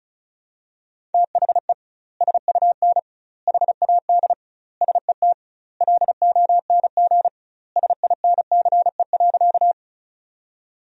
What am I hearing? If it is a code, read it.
Morse code, heard as a keyed tone, about 36 wpm, 700 Hz. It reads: THE SUN HAD SET LONG SINCE.
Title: 2x Word-Spacing - Sentences from Top 300 Words - 35wpm